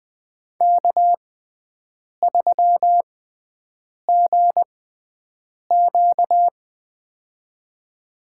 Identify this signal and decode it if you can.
Morse code, heard as a keyed tone, about 21 wpm, spaced as for 7 wpm, 700 Hz. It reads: K3GQ
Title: USA Call Signs - 6x Character Spacing - 20wpm